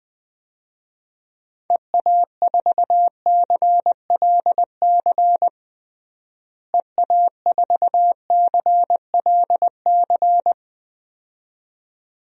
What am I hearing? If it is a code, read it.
Morse code, heard as a keyed tone, about 20 wpm, 700 Hz. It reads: EA4CLC EA4CLC
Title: DXCC-100 Call Signs - 6 Characters - Repeated 2x - 20wpm